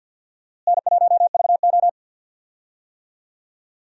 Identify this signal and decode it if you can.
Morse code, heard as a keyed tone, about 46 wpm, 700 Hz. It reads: N1VO
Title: USA Call Signs - 4 Characters - Repeated 2x - 50wpm